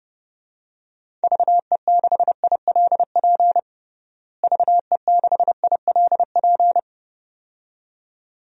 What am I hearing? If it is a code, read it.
Morse code, heard as a keyed tone, about 30 wpm, 700 Hz. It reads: VE6ILP VE6ILP
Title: DXCC-100 Call Signs - 6 Characters - Repeated 2x - 30wpm